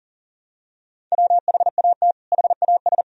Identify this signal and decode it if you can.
Morse code, heard as a keyed tone, about 40 wpm, 700 Hz. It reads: WHAT HAS
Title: Sentences from Top 500 Words and Abbreviations 40wpm